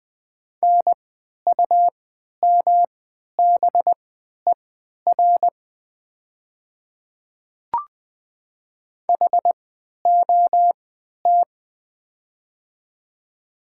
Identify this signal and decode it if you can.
Morse code, heard as a keyed tone, about 20 wpm, spaced as for 11 wpm, 690 Hz. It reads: NUMBER HOT
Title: 3x Character Spacing - Top 100 Words - 20wpm